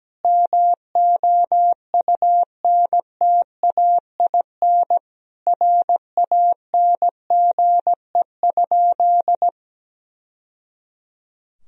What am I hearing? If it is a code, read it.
Morse code, heard as a keyed tone, about 17 wpm, 700 Hz. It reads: MOUNTAIN RANGE?
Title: Sentences from Top 1000 Words - Review Part 4 - 17wpm